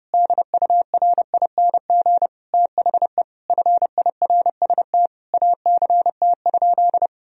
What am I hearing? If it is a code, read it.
Morse code, heard as a keyed tone, about 30 wpm, 700 Hz. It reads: DURING THE FIRST ACT?
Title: Sentences from Top 1000 Words - Review Part 2 - 30wpm